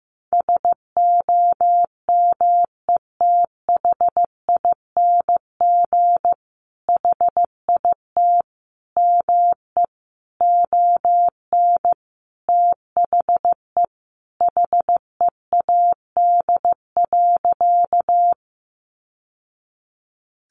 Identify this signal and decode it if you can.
Morse code, heard as a keyed tone, about 15 wpm, 700 Hz. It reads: SOMETHING HIT ME ON THE HEAD.
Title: Sentences from Top 1200 Words 15wpm